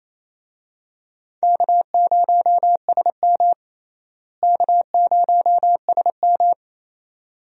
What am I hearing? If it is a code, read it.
Morse code, heard as a keyed tone, about 28 wpm, 700 Hz. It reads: K0SM K0SM
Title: USA Call Signs - 4 Characters - Repeated 2x - 28wpm